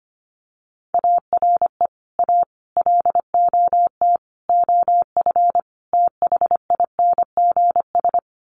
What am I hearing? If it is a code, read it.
Morse code, heard as a keyed tone, about 25 wpm, 700 Hz. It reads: ARE A LOT OF THINGS
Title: Sets of 5 Words Encore2 25wpm